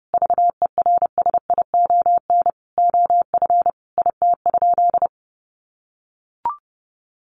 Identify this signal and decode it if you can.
Morse code, heard as a keyed tone, about 30 wpm, 700 Hz. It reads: VERSION OF IT?
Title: Sentences from Top 1400 Words 30wpm